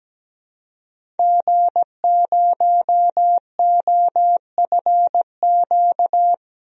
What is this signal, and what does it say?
Morse code, heard as a keyed tone, about 17 wpm, 700 Hz. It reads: G0OFQ
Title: DXCC-100 Call Signs - 5 Characters - Repeated 2x - 17wpm